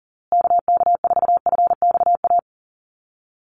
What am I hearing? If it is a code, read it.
Morse code, heard as a keyed tone, about 40 wpm, 700 Hz. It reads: KK4FXA
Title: USA Call Signs Repeated 2x 40wpm